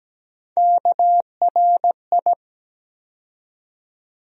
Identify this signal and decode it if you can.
Morse code, heard as a keyed tone, about 17 wpm, 700 Hz. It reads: KRI
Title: USA Call Sign Postfixes - 3 Letters 17wpm